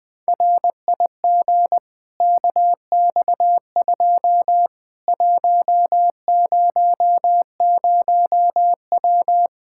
Morse code, heard as a keyed tone, about 21 wpm, 700 Hz. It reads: RIG KX2 100W
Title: Practice QSO - Exchange 2 - Info - 20wpm